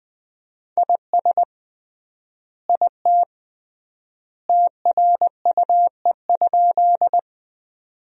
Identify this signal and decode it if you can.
Morse code, heard as a keyed tone, about 20 wpm, 700 Hz. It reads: IS IT TRUE?
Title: 3x Word-Spacing - Sentences from Top 400 Words - 20wpm